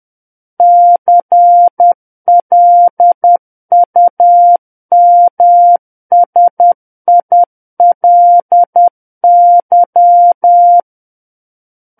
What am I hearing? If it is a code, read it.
Morse code, heard as a keyed tone, about 10 wpm, 700 Hz. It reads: CLUMSILY
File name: Book One / 01-Chapter 1 - 10 WPM